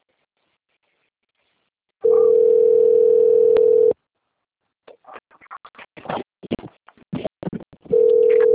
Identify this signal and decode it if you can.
Ringback tone (North America), 440 + 480 Hz.
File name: out- - - - - .14520